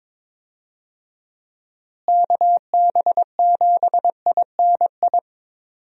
Morse code, heard as a keyed tone, about 22 wpm, 700 Hz. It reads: KB7INI
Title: USA Call Signs - 6 Characters - Repeated 2x - 22wpm